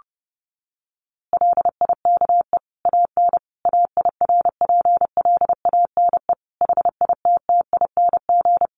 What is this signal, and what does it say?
Morse code, heard as a keyed tone, about 30 wpm, 700 Hz. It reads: LIKE AN AIRPLANE HITTING
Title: A Sound Of Thunder 30wpm